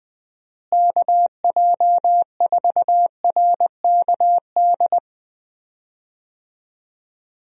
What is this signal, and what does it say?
Morse code, heard as a keyed tone, about 20 wpm, 700 Hz. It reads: KJ4RKD
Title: USA Call Signs - 6 Characters - Repeated 3x - 20wpm